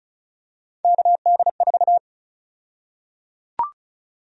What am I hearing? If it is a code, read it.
Morse code, heard as a keyed tone, about 34 wpm, 700 Hz. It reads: KD4
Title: USA Call Sign Prefixes - 35wpm